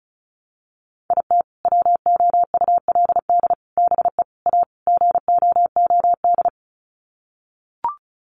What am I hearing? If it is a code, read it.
Morse code, heard as a keyed tone, about 34 wpm, 700 Hz. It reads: IT WOULD BE A GOOD
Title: Sets of 5 Words Encore2 35wpm